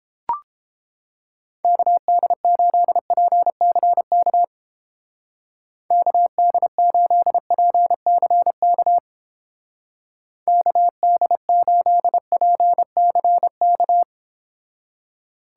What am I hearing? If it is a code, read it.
Morse code, heard as a keyed tone, about 31 wpm, 700 Hz. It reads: KD8PCK KD8PCK KD8PCK